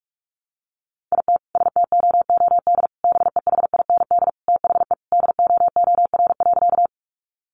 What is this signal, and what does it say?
Morse code, heard as a keyed tone, about 44 wpm, 700 Hz. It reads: IT STOOD BEHIND THE DOOR.
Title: Sentences from Top 500 Words 45wpm